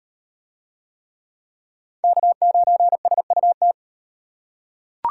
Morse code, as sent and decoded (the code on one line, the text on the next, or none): -.- ----. ... ..- -
K9SUT